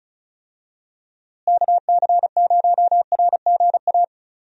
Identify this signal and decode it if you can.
Morse code, heard as a keyed tone, about 34 wpm, 700 Hz. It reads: KC0RGA
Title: USA Call Signs - 6 Characters - Repeated 2x - 35wpm